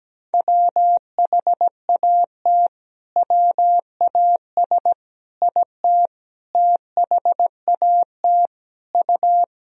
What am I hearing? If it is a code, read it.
Morse code, heard as a keyed tone, about 17 wpm, 700 Hz. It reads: WHAT WAS IT THAT U
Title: Sentences from Top 300 Words and Abbreviations 17wpm